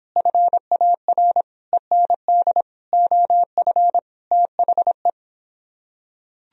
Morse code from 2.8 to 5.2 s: OF THE

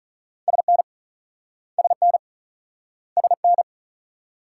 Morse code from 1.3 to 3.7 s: SN SN

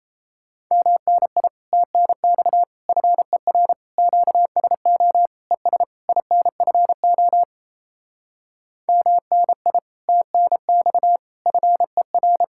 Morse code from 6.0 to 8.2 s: INFO

